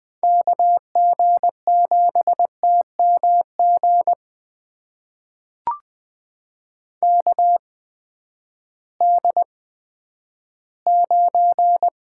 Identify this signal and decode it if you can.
Morse code, heard as a keyed tone, about 20 wpm, spaced as for 11 wpm, 700 Hz. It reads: KG7TMG KD9
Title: USA Call Signs - 8x Character Spacing - 20wpm